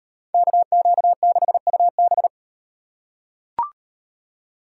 Morse code, heard as a keyed tone, about 36 wpm, 700 Hz. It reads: KQ6UB